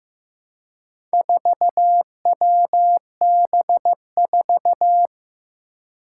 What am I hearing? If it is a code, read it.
Morse code, heard as a keyed tone, about 15 wpm, 700 Hz. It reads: 4WB4
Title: Four Challenging Letter-Number Combinations 15wpm